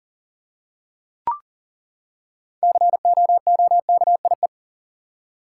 Morse code, heard as a keyed tone, about 40 wpm, 700 Hz. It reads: COOKIE
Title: Six Letter Words - 40wpm